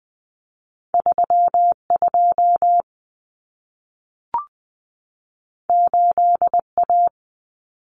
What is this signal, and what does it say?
Morse code, heard as a keyed tone, about 20 wpm, 700 Hz. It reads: 3 2 8 A
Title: Common Cut Numbers - 2-Digit 20wpm